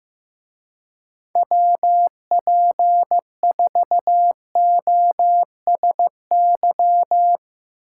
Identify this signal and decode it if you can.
Morse code, heard as a keyed tone, about 15 wpm, 700 Hz. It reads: WP4OSY
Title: Speed Racing - USA 2x3 Call Signs 15wpm